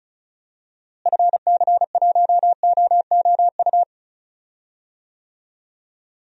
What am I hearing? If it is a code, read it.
Morse code, heard as a keyed tone, about 34 wpm, 700 Hz. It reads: FC1OOU
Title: DXCC-100 Call Signs - 6 Characters - Repeated 3x - 35wpm